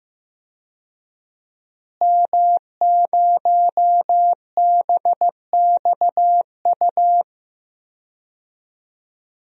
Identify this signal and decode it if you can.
Morse code, heard as a keyed tone, about 15 wpm, 700 Hz. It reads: M0BXU
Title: DXCC-100 Call Signs - 5 Characters - Repeated 3x - 15wpm